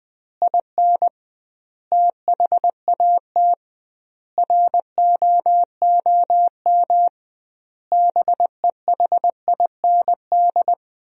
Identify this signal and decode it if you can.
Morse code, heard as a keyed tone, about 20 wpm, 700 Hz. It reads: IN THAT ROOM BEHIND
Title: 2x Word-Spacing - Sentences from Top 500 Words - 20wpm